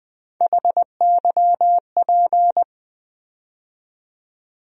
Morse code, heard as a keyed tone, about 20 wpm, 700 Hz. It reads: HYP
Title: USA Call Sign Postfixes - 3 Letters 20wpm